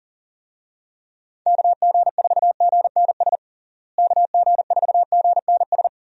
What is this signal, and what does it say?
Morse code, heard as a keyed tone, about 40 wpm, 700 Hz. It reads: KG4GNS KG4GNS
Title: USA Call Signs - 6 Characters - Repeated 2x - 40wpm